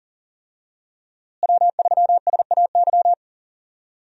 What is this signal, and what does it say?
Morse code, heard as a keyed tone, about 40 wpm, 700 Hz. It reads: W3SAY
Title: USA Call Signs - 5 Characters - Repeated 2x - 40wpm